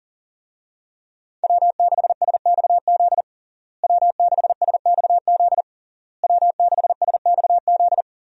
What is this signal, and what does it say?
Morse code, heard as a keyed tone, about 40 wpm, 700 Hz. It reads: W6SXZ W6SXZ W6SXZ
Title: USA Call Signs - 5 Characters - Repeated 3x - 40wpm